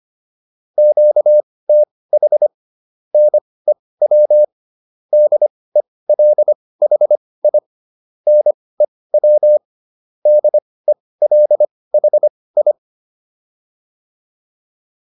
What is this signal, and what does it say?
Morse code, heard as a keyed tone, about 25 wpm, spaced as for 18 wpm, 600 Hz. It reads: QTH NEW DELHI NEW DELHI
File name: QSO410 18